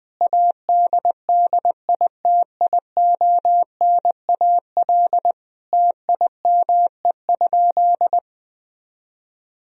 Morse code, heard as a keyed tone, about 20 wpm, 700 Hz. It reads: ADDITIONAL TIME?
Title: Sentences from Top 1000 Words - Review Part 4 - 20wpm